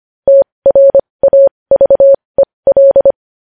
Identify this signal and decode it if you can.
Morse code, heard as a keyed tone, about 25 wpm, spaced as for 20 wpm, 550 Hz. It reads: TRAVEL
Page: news-headlines-in-morse-code-at-20-wpm